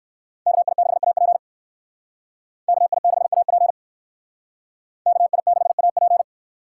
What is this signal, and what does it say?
Morse code, heard as a keyed tone, about 57 wpm, 690 Hz. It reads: KI6AP KI6AP KI6AP